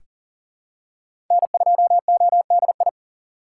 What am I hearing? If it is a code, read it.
Morse code, heard as a keyed tone, about 40 wpm, 700 Hz. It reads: N2ODI
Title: USA Call Signs - 5 Characters - 3x Character Spacing - 40wpm